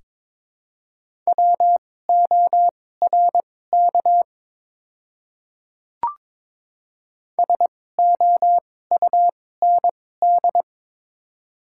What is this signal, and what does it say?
Morse code, heard as a keyed tone, about 22 wpm, spaced as for 16 wpm, 700 Hz. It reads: WORK SOUND
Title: 2x Character Spacing - Top 200 Words - 22wpm